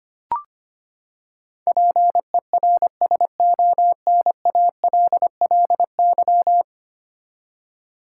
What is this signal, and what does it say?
Morse code, heard as a keyed tone, about 25 wpm, 700 Hz. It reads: PERSONALLY
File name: Ten Letter Words - 25wpm